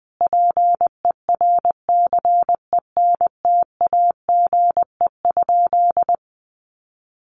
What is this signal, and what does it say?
Morse code, heard as a keyed tone, about 20 wpm, 700 Hz. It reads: PERCENTAGE?
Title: Sentences from Top 1300 Words 20wpm